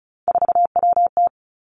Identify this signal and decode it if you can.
Morse code, heard as a keyed tone, about 36 wpm, 700 Hz. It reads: 4WT
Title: Three Challenging Letter-Number Combinations 35wpm